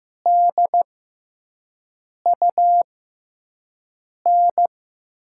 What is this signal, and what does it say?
Morse code, heard as a keyed tone, about 15 wpm, spaced as for 5 wpm, 700 Hz. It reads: DUN
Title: USA Call Signs - 5 Characters - 6x Character Spacing - 15wpm